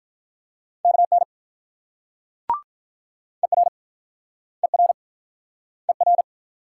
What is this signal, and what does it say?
Morse code, heard as a keyed tone, about 48 wpm, 690 Hz. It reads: KN ER ER ER